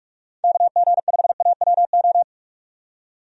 Morse code, heard as a keyed tone, about 41 wpm, 700 Hz. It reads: KG5AWO
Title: USA Call Signs - 6 Characters - 45wpm